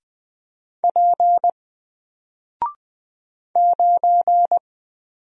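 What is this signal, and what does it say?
Morse code, heard as a keyed tone, about 20 wpm, 700 Hz. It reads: P 9
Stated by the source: Single letter and number Morse drill at 20 wpm, from N4LCD.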